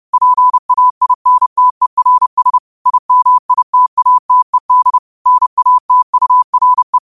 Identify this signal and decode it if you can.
Morse code, heard as a keyed tone, about 30 wpm, 1000 Hz. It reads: PAINTERS IMITATED NATURE